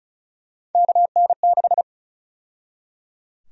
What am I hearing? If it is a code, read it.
Morse code, heard as a keyed tone, about 36 wpm, 700 Hz. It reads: KN6
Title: USA Call Sign Prefixes - 35wpm